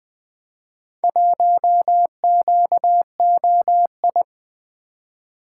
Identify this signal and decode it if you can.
Morse code, heard as a keyed tone, about 20 wpm, 700 Hz. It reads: 1QOI
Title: Four Letter-Number Combinations - 20wpm